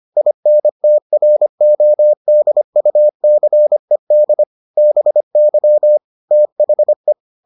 Morse code, heard as a keyed tone, about 25 wpm, 600 Hz. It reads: INTRODUCED BY THE